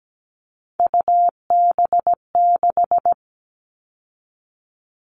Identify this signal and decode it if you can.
Morse code, heard as a keyed tone, about 17 wpm, 700 Hz. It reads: UB6
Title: All Cut Numbers - 3-Digit 17wpm